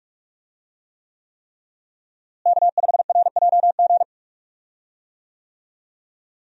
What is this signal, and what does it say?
Morse code, heard as a keyed tone, about 41 wpm, 700 Hz. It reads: K5RJG
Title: USA Call Signs - 5 Characters - Repeated 3x - 45wpm